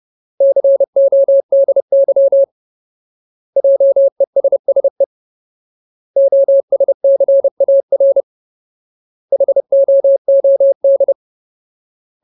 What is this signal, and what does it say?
Morse code, heard as a keyed tone, about 30 wpm, 550 Hz. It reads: CODY JESSE OSCAR HOOD
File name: Ultimate Word List / 30WPM Ultimate